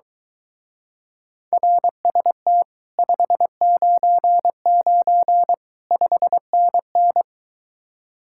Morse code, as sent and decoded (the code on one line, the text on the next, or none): .-. ... - / ..... ----. ----. / ..... -. -.
RST 599 5NN